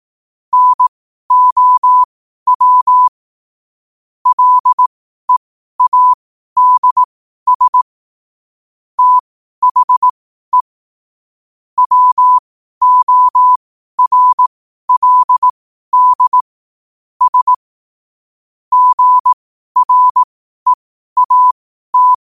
Morse code, heard as a keyed tone, about 18 wpm, spaced as for 12 wpm, 1000 Hz. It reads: NOW LEADS THE WORLD S GREAT
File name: qotdfiles / 12wpm